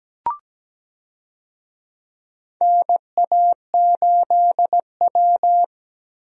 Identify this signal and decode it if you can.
Morse code, heard as a keyed tone, about 17 wpm, 700 Hz. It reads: NA8W
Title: USA Call Signs - 4 Characters - Repeated 3x - 17wpm